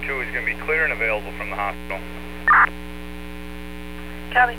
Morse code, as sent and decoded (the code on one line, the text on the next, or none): none